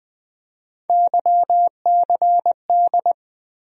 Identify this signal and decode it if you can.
Morse code, heard as a keyed tone, about 20 wpm, 700 Hz. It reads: YCD
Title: USA Call Sign Postfixes - 3 Letters 20wpm